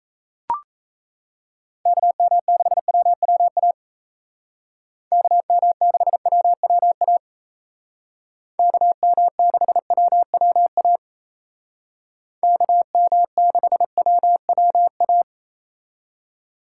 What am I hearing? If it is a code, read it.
Morse code, heard as a keyed tone, about 36 wpm, 700 Hz. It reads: KM6WWA KM6WWA KM6WWA KM6WWA